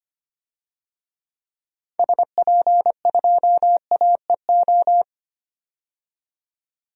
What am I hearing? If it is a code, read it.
Morse code, heard as a keyed tone, about 25 wpm, 700 Hz. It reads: SP2AEO
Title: DXCC-100 Call Signs - 6 Characters - 25wpm